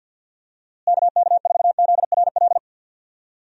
Morse code, heard as a keyed tone, about 51 wpm, 700 Hz. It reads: KK4ZRL
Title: USA Call Signs - 6 Characters - Repeated 2x - 50wpm